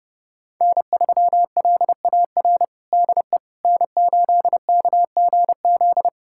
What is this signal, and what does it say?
Morse code, heard as a keyed tone, about 30 wpm, 700 Hz. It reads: N3LAR DE N8KGZ
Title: USA Callsign DE Callsign - 5 Characters - 30wpm